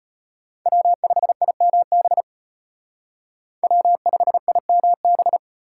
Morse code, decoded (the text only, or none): W5IMB W5IMB